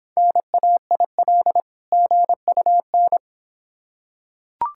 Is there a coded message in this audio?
Yes: Morse code from 0.2 to 3.2 s, heard as a keyed tone, about 26 wpm, 700 Hz. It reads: NAIL GUN